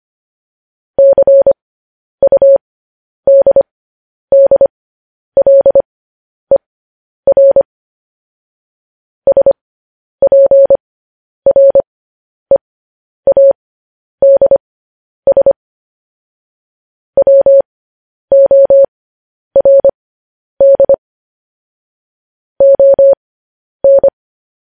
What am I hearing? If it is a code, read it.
Morse code, heard as a keyed tone, about 25 wpm, spaced as for 10 wpm, 550 Hz. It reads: CUDDLER SPREADS WORD ON
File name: podcast / News headlines-1x-25x10